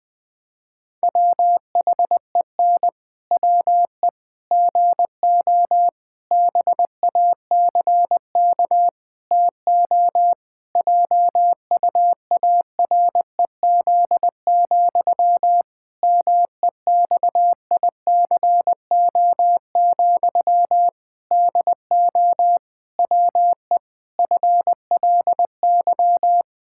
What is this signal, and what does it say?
Morse code, heard as a keyed tone, about 21 wpm, 700 Hz. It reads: WHEN WE GO BACK TO JUAREZ, MEXICO, DO WE FLY
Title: Pangram Challenge v2 - 20wpm